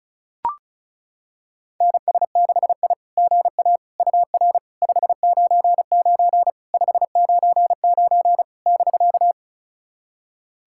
Morse code, heard as a keyed tone, about 36 wpm, 690 Hz. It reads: NS6I GA UR 599 599 <BK>
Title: SOTA QSO Practice 35wpm